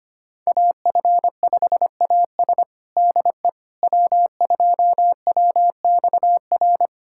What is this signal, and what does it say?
Morse code, heard as a keyed tone, about 25 wpm, 700 Hz. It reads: AF5AS DE W2WXR
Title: USA Callsign DE Callsign - 5 Characters - 25wpm